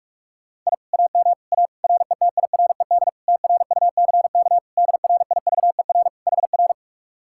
Morse code, heard as a keyed tone, about 44 wpm, 700 Hz. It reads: I AM A RETIRED TRUCK DRIVER HR